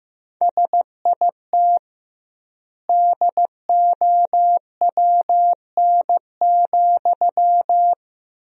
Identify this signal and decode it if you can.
Morse code, heard as a keyed tone, about 15 wpm, 700 Hz. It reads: SIT DOWN,
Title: 2x Word-Spacing - Sentences from Top 500 Words - 15wpm